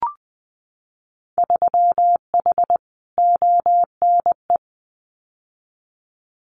Morse code, heard as a keyed tone, about 20 wpm, 700 Hz. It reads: 3H ONE
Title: Winter Field Day Exchanges Encore 20wpm